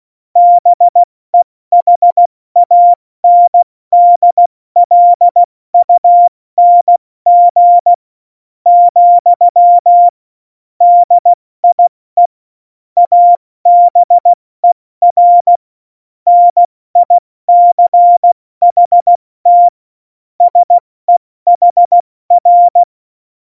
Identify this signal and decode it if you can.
Morse code, heard as a keyed tone, about 16 wpm, 700 Hz. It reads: BEHANDLUNG , DIE ABER NICHT SEHR
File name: Morsekurs / Witze / Witz 13 March 2018, 70 F80